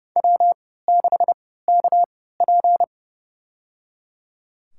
Morse code, heard as a keyed tone, about 30 wpm, spaced as for 17 wpm, 700 Hz. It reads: W6KP
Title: USA Call Signs - 4 Characters - 3x Character Spacing - 30wpm